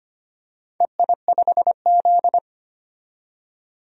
Morse code, heard as a keyed tone, about 25 wpm, 700 Hz. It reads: EI5Z
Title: DXCC-100 Call Signs - 4 Characters - 25wpm